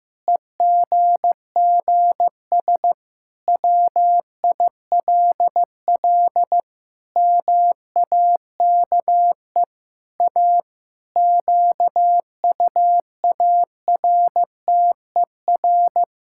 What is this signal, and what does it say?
Morse code, heard as a keyed tone, about 16 wpm, 700 Hz. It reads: EGGS WILL MAKE A QUARTER
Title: Pangram Challenge v2 - 15wpm